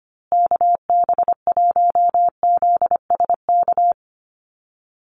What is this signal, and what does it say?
Morse code, heard as a keyed tone, about 25 wpm, 700 Hz. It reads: KB1ZSK
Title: USA Call Signs Repeated 2x 25wpm